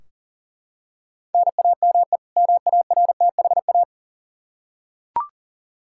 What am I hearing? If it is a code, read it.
Morse code, heard as a keyed tone, about 40 wpm, 700 Hz. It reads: NAME MARTHA
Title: Practice QSO - Exchange 1 - Beginning - 40wpm